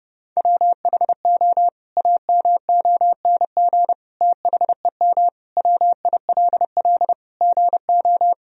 Morse code, heard as a keyed tone, about 32 wpm, 700 Hz. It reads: WHO AMONG THEM WILL GO
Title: Sentences from Top 1000 Words - Review Part 3 - 30wpm